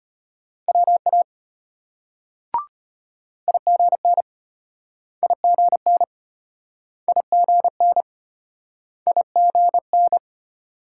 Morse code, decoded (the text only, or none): WA IGN IGN IGN IGN